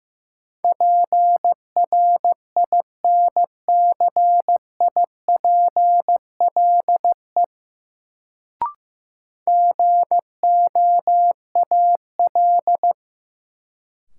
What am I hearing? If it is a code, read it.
Morse code, heard as a keyed tone, about 15 wpm, 700 Hz. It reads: PRINCIPLE GOAL
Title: Sentences from Top 1400 Words - New Word Drill Alt - 15wpm